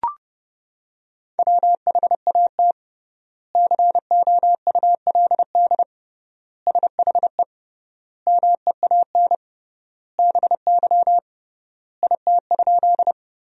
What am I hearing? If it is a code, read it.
Morse code, heard as a keyed tone, about 32 wpm, 700 Hz. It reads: WHAT COULD SHE MEAN BY IT?
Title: 3x Word-Spacing - Sentences from Top 200 Words - 30wpm